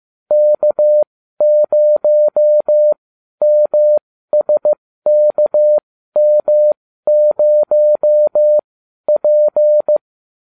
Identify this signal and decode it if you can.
Morse code, heard as a keyed tone, about 15 wpm, spaced as for 12 wpm, 600 Hz. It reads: K0MSKM0P